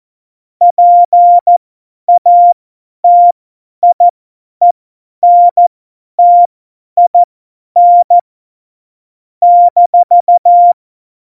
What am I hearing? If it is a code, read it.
Morse code, heard as a keyed tone, about 14 wpm, spaced as for 10 wpm, 700 Hz. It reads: PATIENTIN -
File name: Morsekurs / Witze / Witz 06. Februar 2018, 50 F70